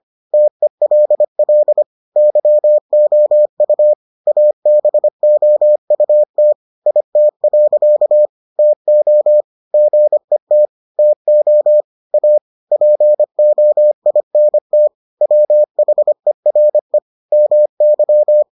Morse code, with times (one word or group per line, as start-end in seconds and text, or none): 0.3-1.8 TELL
2.2-3.9 YOU
4.3-6.5 ABOUT
6.9-8.3 IT.
8.6-9.4 TO
9.7-10.7 GET
11.0-11.8 TO
12.1-12.4 A
12.7-14.9 POINT
15.2-17.0 WHERE
17.3-18.4 MY